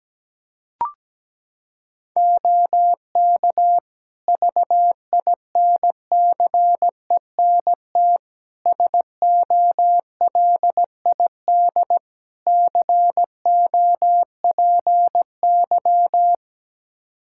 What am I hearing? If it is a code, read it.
Morse code, heard as a keyed tone, about 17 wpm, 700 Hz. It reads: OK VINCENT SOLID COPY
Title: Practice QSO - Exchange 3 - More Info - 17wpm